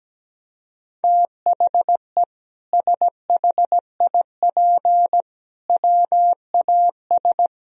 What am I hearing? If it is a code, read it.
Morse code, heard as a keyed tone, about 17 wpm, 700 Hz. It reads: THE SHIP WAS
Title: Sentences from Top 1000 Words - Review Part 3 - 17wpm